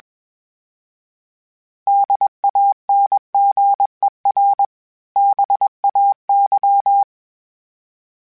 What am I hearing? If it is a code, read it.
Morse code, heard as a keyed tone, about 21 wpm, 800 Hz. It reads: DANGER BAY
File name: fromSMs / FromSMs mp3s / Cities@20